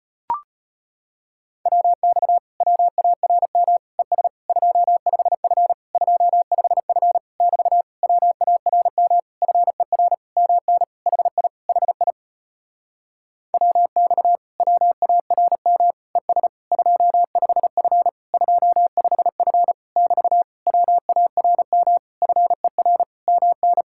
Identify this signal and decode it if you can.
Morse code, heard as a keyed tone, about 37 wpm, 700 Hz. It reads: WX WARM ES 25F 25F <BT> WARM FER MN HI HI WX WARM ES 25F 25F <BT> WARM FER MN